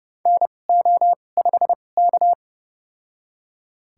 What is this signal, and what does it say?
Morse code, heard as a keyed tone, about 30 wpm, spaced as for 22 wpm, 700 Hz. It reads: NO5K
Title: USA Call Signs - 4 Characters - 2x Character Spacing - 30wpm